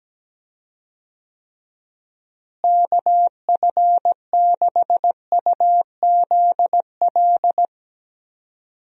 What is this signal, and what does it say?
Morse code, heard as a keyed tone, about 17 wpm, 700 Hz. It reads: KF6UZL
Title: USA Call Signs - 6 Characters - Repeated 2x - 17wpm